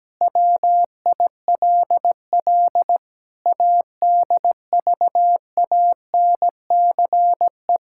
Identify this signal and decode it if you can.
Morse code, heard as a keyed tone, about 17 wpm, 700 Hz. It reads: WILL ADVANCE
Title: Sentences from Top 1600 Words 17wpm